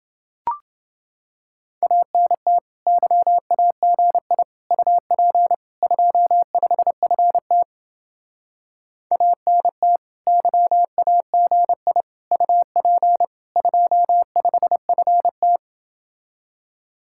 Morse code, heard as a keyed tone, about 30 wpm, 700 Hz. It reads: ANT YAGI UP 25FT ANT YAGI UP 25FT